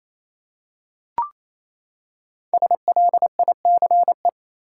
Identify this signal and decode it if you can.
Morse code, heard as a keyed tone, about 28 wpm, 700 Hz. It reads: SLICE
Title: Five Letter Words - 28wpm